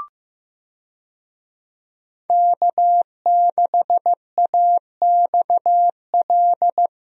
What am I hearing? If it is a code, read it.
Morse code, heard as a keyed tone, about 15 wpm, 700 Hz. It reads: K6AXL
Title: USA Call Signs - 5 Characters - Repeated 3x - 15wpm